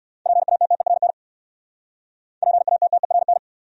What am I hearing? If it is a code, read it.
Morse code, heard as a keyed tone, about 66 wpm, 690 Hz. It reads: PATTERN PATTERN